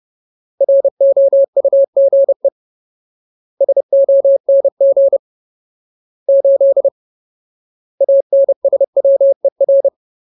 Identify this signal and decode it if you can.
Morse code, heard as a keyed tone, about 30 wpm, 550 Hz. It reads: ROUGE SONG 8 ANSWER
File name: Ultimate Word List / 30WPM Ultimate